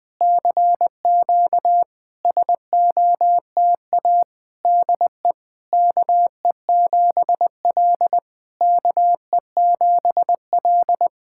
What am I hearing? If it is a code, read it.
Morse code, heard as a keyed tone, about 20 wpm, 690 Hz. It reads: CQ SOTA DE KE7L KE7L
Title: SOTA QSO Practice 20wpm